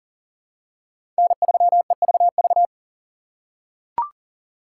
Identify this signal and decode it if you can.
Morse code, heard as a keyed tone, about 39 wpm, 700 Hz. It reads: N3EVV
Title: USA Call Signs - 5 Characters - 4x Character Spacing - 40wpm